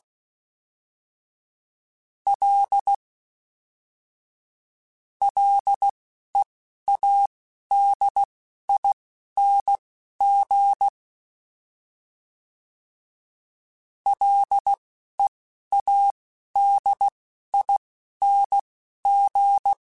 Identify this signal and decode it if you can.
Morse code, heard as a keyed tone, about 16 wpm, spaced as for 11 wpm, 800 Hz. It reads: L LEADING LEADING
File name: F and L / F and L mp3 / F and L Prefixes 3x2 Twice @10